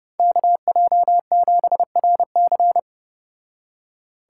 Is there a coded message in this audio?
Yes: Morse code from 0.2 to 2.8 s, heard as a keyed tone, about 34 wpm, 700 Hz. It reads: KJ7RC